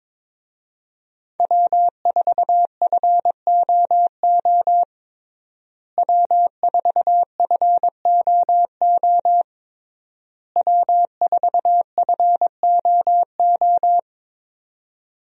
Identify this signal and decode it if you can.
Morse code, heard as a keyed tone, about 21 wpm, 700 Hz. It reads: W4FOO W4FOO W4FOO
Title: USA Call Signs - 5 Characters - Repeated 3x - 22wpm